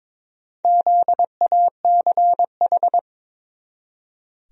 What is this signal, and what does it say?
Morse code, heard as a keyed tone, about 22 wpm, 700 Hz. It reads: ZACH
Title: USA Common Names - Max 4 Letters - 22wpm